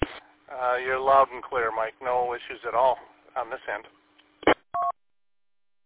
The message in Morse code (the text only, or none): none